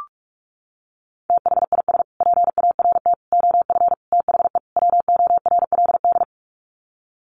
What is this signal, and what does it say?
Morse code, heard as a keyed tone, about 44 wpm, 700 Hz. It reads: THIS PART OF THE WORLD
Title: Sets of 5 Words Encore2 45wpm